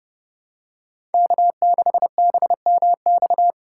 Morse code, heard as a keyed tone, about 30 wpm, 700 Hz. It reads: K6BMX
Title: USA Call Signs - 4x Character Spacing - 30wpm